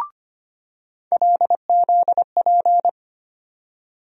Morse code, heard as a keyed tone, about 25 wpm, 700 Hz. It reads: LZP
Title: USA Call Sign Postfixes - 3 Letters 25wpm